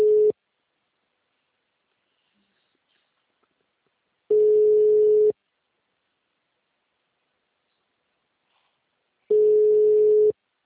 Ringback tone (Europe, 425 Hz).
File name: out- -2031- - - .64854